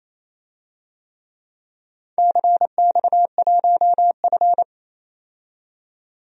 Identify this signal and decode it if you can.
Morse code, heard as a keyed tone, about 28 wpm, 700 Hz. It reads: CX1F
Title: DXCC-100 Call Signs - 4 Characters - 28wpm